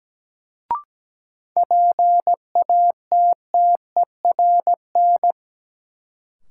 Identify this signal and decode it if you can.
Morse code, heard as a keyed tone, about 17 wpm, 700 Hz. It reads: PATTERN
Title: Sentences from Top 1000 Words - New Word Drill - 17wpm